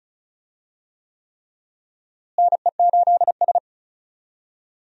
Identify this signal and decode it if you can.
Morse code, heard as a keyed tone, about 36 wpm, 700 Hz. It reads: NE8S
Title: USA Call Signs - 4 Characters - Repeated 3x - 35wpm